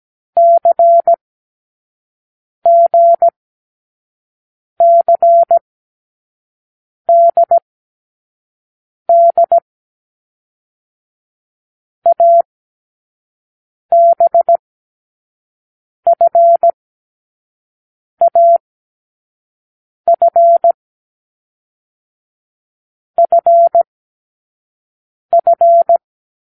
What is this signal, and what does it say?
Morse code, heard as a keyed tone, about 17 wpm, 680 Hz. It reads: C G C D D A B F A F F F
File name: k7qo / Lesson12(A-H)